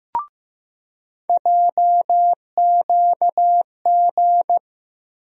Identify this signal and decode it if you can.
Morse code, heard as a keyed tone, about 15 wpm, 700 Hz. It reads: JQG
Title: USA Call Sign Postfixes - 3 Letters 15wpm